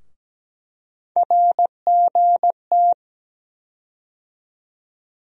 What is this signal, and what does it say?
Morse code, heard as a keyed tone, about 17 wpm, 700 Hz. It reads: RGT
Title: USA Call Sign Postfixes - 3 Letters 17wpm